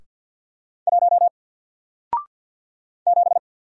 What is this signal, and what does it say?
Morse code, heard as a keyed tone, about 48 wpm, 700 Hz. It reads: 1 7
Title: Single Letter-Number Alt 50wpm